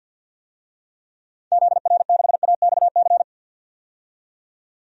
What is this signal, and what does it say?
Morse code, heard as a keyed tone, about 47 wpm, 700 Hz. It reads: ZR6AXC
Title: DXCC-100 Call Signs - 6 Characters - 50wpm